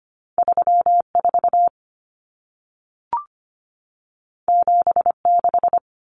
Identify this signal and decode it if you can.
Morse code, heard as a keyed tone, about 25 wpm, 700 Hz. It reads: 34 76
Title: Common Cut Numbers - 2-Digit 25wpm